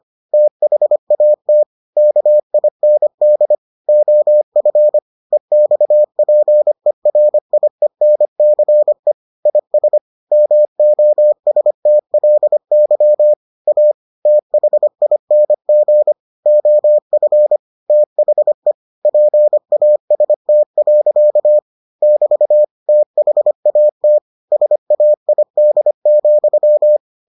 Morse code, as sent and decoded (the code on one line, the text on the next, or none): - .... .- - / -.- .. -. -.. / --- ..-. / . -..- .--. .-. .. -. -.-. / .. ... / -- --- ... - .-.. -.-- / .- / - .... .. -. --. / --- ..-. / - .... / .--. .- ... - .-.-.- / -...- / - .... .- - / ... .- .. -.. --..--
THAT KIND OF EXPERIENCE IS MOSTLY A THING OF THE PAST. <BT> THAT SAID,